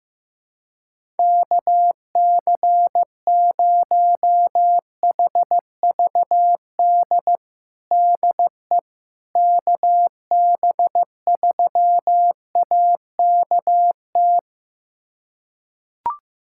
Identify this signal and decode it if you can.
Morse code, heard as a keyed tone, about 15 wpm, 700 Hz. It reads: KC0HVD DE KB3AKT
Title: USA Callsign DE Callsign 15wpm